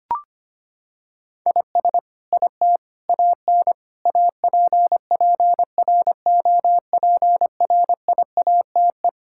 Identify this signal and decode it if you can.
Morse code, heard as a keyed tone, about 25 wpm, 700 Hz. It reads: IS IT AN APPROPRIATE
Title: Sentences from Top 1000 Words - Review Part 5 - 25wpm